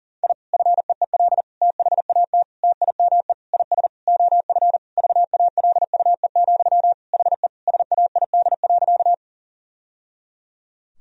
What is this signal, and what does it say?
Morse code, heard as a keyed tone, about 40 wpm, 700 Hz. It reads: I FEEL THAT TIME IS OF VALUE, HE SAID.
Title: Sentences from Top 1000 Words - Review Part 3 - 40wpm